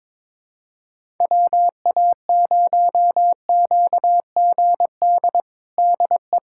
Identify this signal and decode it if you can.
Morse code, heard as a keyed tone, about 22 wpm, 700 Hz. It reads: WA0QGD DE
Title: USA Callsign DE Callsign 22wpm